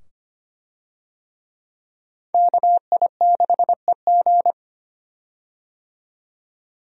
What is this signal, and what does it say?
Morse code, heard as a keyed tone, about 25 wpm, 700 Hz. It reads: KI6EG
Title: USA Call Signs - 5 Characters - 25wpm